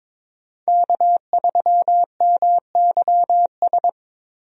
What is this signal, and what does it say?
Morse code, heard as a keyed tone, about 22 wpm, 700 Hz. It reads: K3MYS
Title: USA Call Signs - 5 Characters - 22wpm